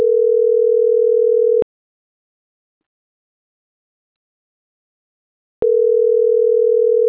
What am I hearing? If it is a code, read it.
Ringback tone (North America), 440 + 480 Hz.
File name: CENTRAL MAYAB / out atzhyry.gomez